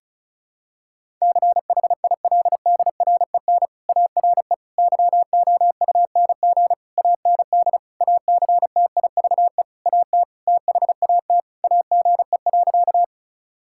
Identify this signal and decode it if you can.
Morse code, heard as a keyed tone, about 34 wpm, 700 Hz. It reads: CHILDREN ARE YOUNG AND ACTIVE AT THAT AGE.
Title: Sentences from Top 1000 Words - Review Part 6 - 35wpm